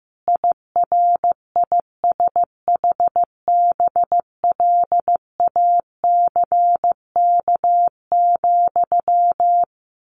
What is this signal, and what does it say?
Morse code, heard as a keyed tone, about 15 wpm, 700 Hz. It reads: IRISHBLACK,
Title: A Sound Of Thunder 15wpm